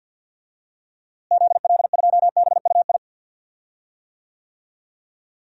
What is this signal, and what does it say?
Morse code, heard as a keyed tone, about 50 wpm, 700 Hz. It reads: ZL2BUI
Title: DXCC-100 Call Signs - 6 Characters - Repeated 2x - 50wpm